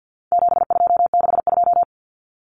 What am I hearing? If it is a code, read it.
Morse code, heard as a keyed tone, about 50 wpm, 700 Hz. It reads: 7262
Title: Common Cut Numbers - 4-Digit 50wpm